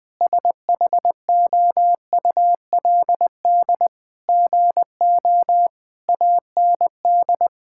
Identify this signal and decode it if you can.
Morse code, heard as a keyed tone, about 20 wpm, 700 Hz. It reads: SHOULD GO AND
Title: Sentences from Top 1600 Words 20wpm